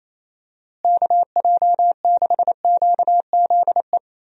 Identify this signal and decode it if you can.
Morse code, heard as a keyed tone, about 28 wpm, 700 Hz. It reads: KJ6QZE
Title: USA Call Signs - 6 Characters - Repeated 2x - 28wpm